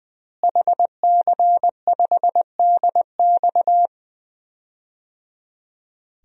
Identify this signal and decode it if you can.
Morse code, heard as a keyed tone, about 21 wpm, 700 Hz. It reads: HC5DX
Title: DXCC-100 Call Signs - 5 Characters - Repeated 3x - 20wpm